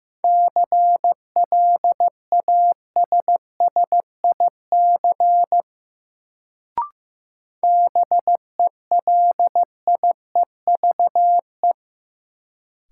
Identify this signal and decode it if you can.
Morse code, heard as a keyed tone, about 15 wpm, 700 Hz. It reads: CLASSIC BELIEVE
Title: Sentences from Top 1000 Words - New Word Drill - 15wpm